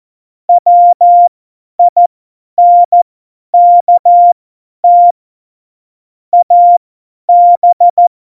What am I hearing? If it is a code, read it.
Morse code, heard as a keyed tone, about 14 wpm, spaced as for 10 wpm, 700 Hz. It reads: WINKT AB
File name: Morsekurs / Witze / Witz 12 Mai 2020, 50 F70